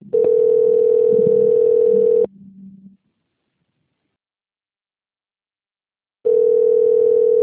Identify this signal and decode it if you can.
Ringback tone (North America), 440 + 480 Hz.
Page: Nova Odessa - SP